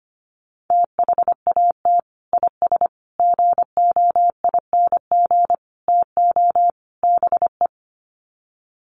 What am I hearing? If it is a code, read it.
Morse code, heard as a keyed tone, about 25 wpm, 700 Hz. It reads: THAT IS GOING TO BE
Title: Sets of 5 Words Encore2 25wpm